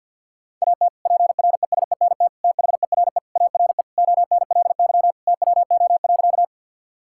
Morse code, heard as a keyed tone, about 50 wpm, 700 Hz. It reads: AT PRESENT THERE ARE ONLY TWO.
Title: Sentences from Top 1000 Words - Review Part 5 - 50wpm